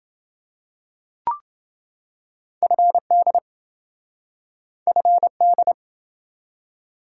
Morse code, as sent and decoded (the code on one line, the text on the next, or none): ..-. -.. / ..-. -..
FD FD